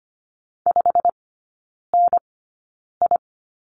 Morse code, heard as a keyed tone, about 25 wpm, spaced as for 9 wpm, 700 Hz. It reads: 5NI
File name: Lesson 17e - Letter u - Three Characters 25wpm